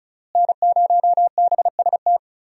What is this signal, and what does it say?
Morse code, heard as a keyed tone, about 36 wpm, 700 Hz. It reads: N0BST
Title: USA Call Signs - 5 Characters - Repeated 2x - 35wpm